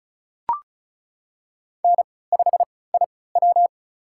Morse code, heard as a keyed tone, about 36 wpm, spaced as for 20 wpm, 700 Hz. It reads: N5IW
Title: USA Call Signs - 4 Characters - 3x Character Spacing - 35wpm